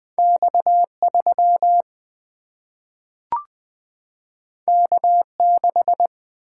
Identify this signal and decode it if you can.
Morse code, heard as a keyed tone, about 20 wpm, 700 Hz. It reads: X 3 K 6
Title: Single-Letter Single-Number 20wpm